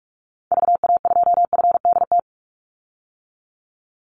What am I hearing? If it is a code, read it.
Morse code, heard as a keyed tone, about 44 wpm, 700 Hz. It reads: VA2FDT
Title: Call Signs - 45wpm